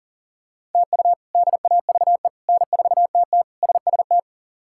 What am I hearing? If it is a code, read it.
Morse code, heard as a keyed tone, about 37 wpm, 700 Hz. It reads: TU DAVE N4TT SST
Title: K1USN SST QSO Practice - 40wpm